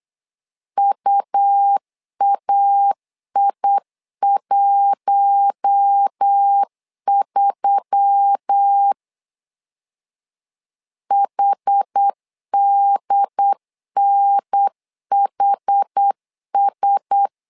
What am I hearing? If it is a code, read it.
Morse code, heard as a keyed tone, about 8 wpm, 800 Hz. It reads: UAI13 HDNHS
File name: LEZIONI / 5 LBVC / 5 L 8wpm 2831